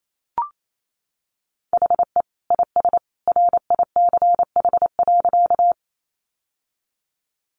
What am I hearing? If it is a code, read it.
Morse code, heard as a keyed tone, about 28 wpm, 700 Hz. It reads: HE IS RICH.
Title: Sentences from Top 1100 Words 28wpm